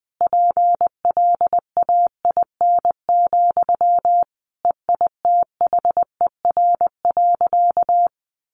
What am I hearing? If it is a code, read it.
Morse code, heard as a keyed tone, about 19 wpm, 700 Hz. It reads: PLAIN, EITHER.
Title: Sentences from Top 600 Words 20wpm